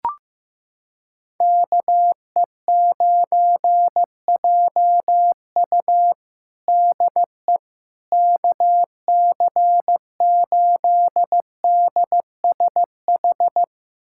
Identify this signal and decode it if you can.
Morse code, heard as a keyed tone, about 15 wpm, 700 Hz. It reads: KE9JU DE KC8DSH
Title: USA Callsign DE Callsign 15wpm